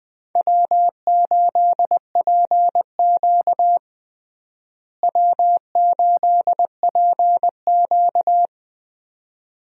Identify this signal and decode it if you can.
Morse code, heard as a keyed tone, about 20 wpm, 700 Hz. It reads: W8PQ W8PQ
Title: USA Call Signs - 4 Characters - Repeated 2x - 20wpm